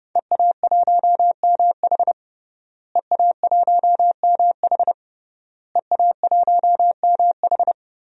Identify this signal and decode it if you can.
Morse code, heard as a keyed tone, about 30 wpm, 700 Hz. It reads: EA1MH EA1MH EA1MH
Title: DXCC-100 Call Signs - 5 Characters - Repeated 3x - 30wpm